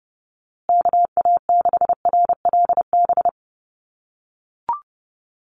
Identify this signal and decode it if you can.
Morse code, heard as a keyed tone, about 30 wpm, 700 Hz. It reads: KA6RLB
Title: USA Call Signs Repeated 3x 30wpm